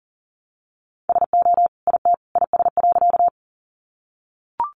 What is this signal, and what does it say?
Morse code, heard as a keyed tone, about 40 wpm, 700 Hz. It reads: SO IT IS.
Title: Sentences from Top 100 Words 40wpm